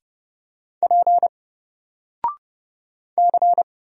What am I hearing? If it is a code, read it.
Morse code, heard as a keyed tone, about 31 wpm, 700 Hz. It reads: P C
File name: Single-Letter-Number-30wpm